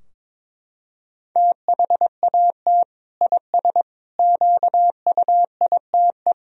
Morse code, heard as a keyed tone, about 22 wpm, 700 Hz. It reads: THAT IS QUITE
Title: Sentences from Top 1000 Words - Review Part 2 - 22wpm